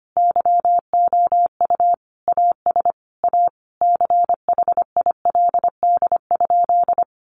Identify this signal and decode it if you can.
Morse code, heard as a keyed tone, about 25 wpm, 700 Hz. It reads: YOU AS A CHILD?
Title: Sentences from Top 1300 Words 25wpm